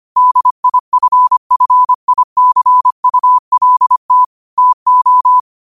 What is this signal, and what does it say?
Morse code, heard as a keyed tone, about 25 wpm, 1000 Hz. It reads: DIFFICULT TO
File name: qotdfiles / 25wpm